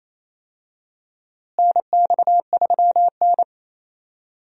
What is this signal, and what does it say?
Morse code, heard as a keyed tone, about 27 wpm, 700 Hz. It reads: NX3N
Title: USA Call Signs - 4 Characters - Repeated 2x - 28wpm